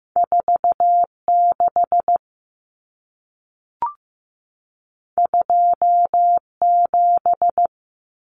Morse code, heard as a keyed tone, about 15 wpm, 700 Hz. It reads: 46 27
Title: Numbers - 2-Digit 15wpm